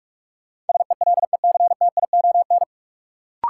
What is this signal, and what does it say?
Morse code, heard as a keyed tone, about 45 wpm, 700 Hz. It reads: SELECTION
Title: Sentences from Top 1400 Words - New Word Drill Alt - 45wpm